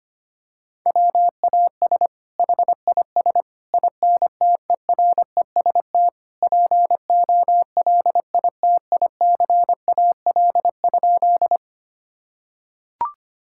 Morse code, heard as a keyed tone, about 25 wpm, 700 Hz. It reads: WAS HIS INTEREST POLITICAL?
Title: Sentences from Top 1000 Words - Review Part 5 - 25wpm